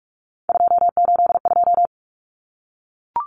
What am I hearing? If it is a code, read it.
Morse code, heard as a keyed tone, about 44 wpm, 700 Hz. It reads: 282
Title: Numbers - 3-Digit 45wpm